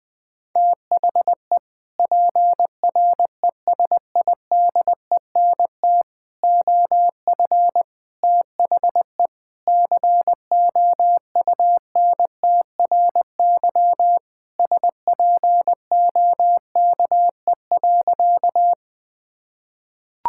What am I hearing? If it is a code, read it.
Morse code, heard as a keyed tone, about 20 wpm, 700 Hz. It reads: THE PRESIDENT OF THE COUNTRY SPOKE.
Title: Sentences with 3-Word Phrases - 20wpm